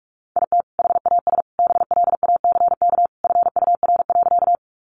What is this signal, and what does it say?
Morse code, heard as a keyed tone, about 44 wpm, 700 Hz. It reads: IT HAS BLACK FUR.
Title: Easy Phrases 45wpm